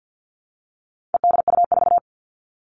Morse code, heard as a keyed tone, about 50 wpm, 700 Hz. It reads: EDU4
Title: All Cut Numbers - 4-Digit 50wpm